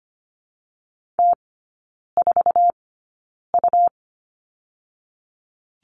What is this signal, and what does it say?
Morse code, heard as a keyed tone, about 25 wpm, 700 Hz. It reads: T 4 U
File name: Lesson 17e - Letter u - Three Characters 25wpm